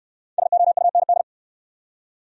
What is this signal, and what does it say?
Morse code, heard as a keyed tone, about 68 wpm, 700 Hz. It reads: SOUND